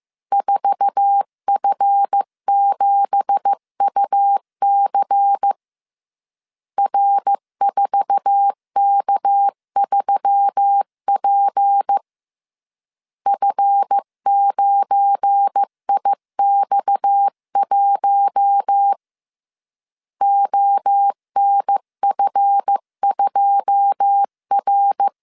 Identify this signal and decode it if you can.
Morse code, heard as a keyed tone, about 15 wpm, 800 Hz. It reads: 4F7UC R4K3P F9IX1 ONF2R
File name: CW / DET / 1 9 1500C S B 14wpm 3302